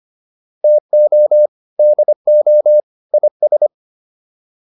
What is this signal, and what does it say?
Morse code, heard as a keyed tone, about 25 wpm, 600 Hz. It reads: TO DO IS